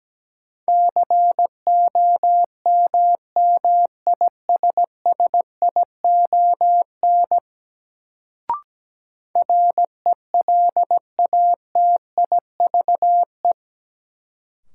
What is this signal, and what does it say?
Morse code, heard as a keyed tone, about 17 wpm, 700 Hz. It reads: COMMISSION RELATIVE
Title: Sentences from Top 1400 Words - New Word Drill - 17wpm